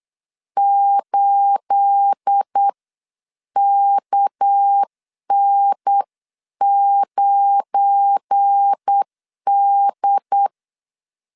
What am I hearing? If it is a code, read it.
Morse code, heard as a keyed tone, about 8 wpm, spaced as for 7 wpm, 800 Hz. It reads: 8KN9D
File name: CW / DET / 1 9 1500C S A 8wpm 5637